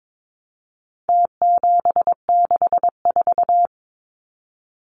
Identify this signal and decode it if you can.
Morse code, heard as a keyed tone, about 22 wpm, 700 Hz. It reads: T764
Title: Numbers - Contest Style - 4-Digit 22wpm